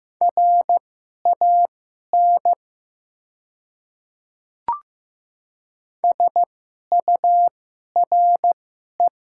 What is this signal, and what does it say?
Morse code, heard as a keyed tone, about 15 wpm, spaced as for 11 wpm, 700 Hz. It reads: RAN SURE
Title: 2x Character Spacing - Top 500 Words - 15wpm